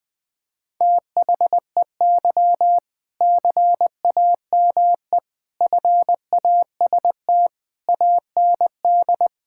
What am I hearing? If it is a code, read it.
Morse code, heard as a keyed tone, about 20 wpm, 700 Hz. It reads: THEY CAME FAST AND
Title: Sentences from Top 1600 Words 20wpm